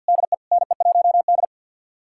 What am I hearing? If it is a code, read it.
Morse code, heard as a keyed tone, about 52 wpm, 700 Hz. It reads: DE NE1D
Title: USA Callsign DE Callsign - 4 Characters - 50wpm